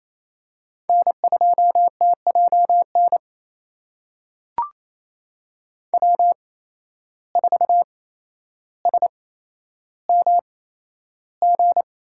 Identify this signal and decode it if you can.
Morse code, heard as a keyed tone, about 28 wpm, spaced as for 12 wpm, 700 Hz. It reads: N2TJN W4SMG
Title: USA Call Signs - 5 Characters - 8x Character Spacing - 28wpm